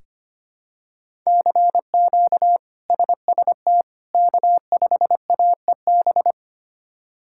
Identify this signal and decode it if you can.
Morse code, heard as a keyed tone, about 25 wpm, 700 Hz. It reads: CQ SST K5AEB
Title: K1USN SST QSO Practice - 25wpm